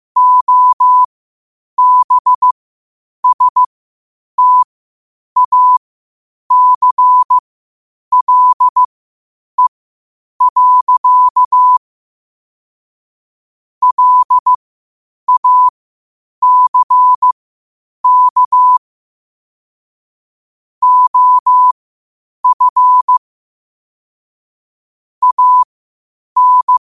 Morse code, heard as a keyed tone, about 15 wpm, spaced as for 8 wpm, 1000 Hz. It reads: OBSTACLE. LACK OF AN
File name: qotdfiles / 8wpm